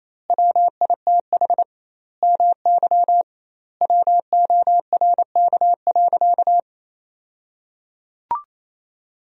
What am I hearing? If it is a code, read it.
Morse code, heard as a keyed tone, about 28 wpm, 700 Hz. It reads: WITH MY WORK.
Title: 2x Word-Spacing - Sentences from Top 300 Words - 28wpm